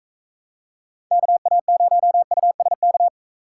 Morse code, heard as a keyed tone, about 45 wpm, 700 Hz. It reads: KA0USK